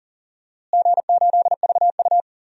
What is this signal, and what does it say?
Morse code, heard as a keyed tone, about 40 wpm, 700 Hz. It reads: G8VU
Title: DXCC-100 Call Signs - 4 Characters - Repeated 2x - 40wpm